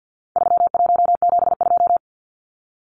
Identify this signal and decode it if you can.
Morse code, heard as a keyed tone, about 50 wpm, 700 Hz. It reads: 3172
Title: Numbers - 4-Digit 50wpm